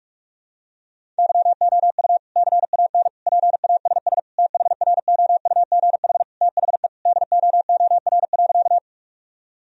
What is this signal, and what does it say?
Morse code, heard as a keyed tone, about 46 wpm, 700 Hz. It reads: YOU CAN PASS THROUGH THE DOOR.
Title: Sentences from Top 1000 Words - Review Part 1 - 45wpm